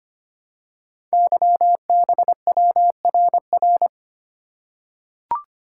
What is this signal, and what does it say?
Morse code, heard as a keyed tone, about 25 wpm, 700 Hz. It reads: YBWRR
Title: Code Groups - 5 Letters - 25wpm